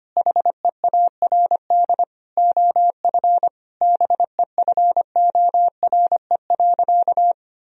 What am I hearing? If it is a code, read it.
Morse code, heard as a keyed tone, about 25 wpm, 700 Hz. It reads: HEARD OF BEFORE.
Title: Sentences from Top 1000 Words - Review Part 3 - 25wpm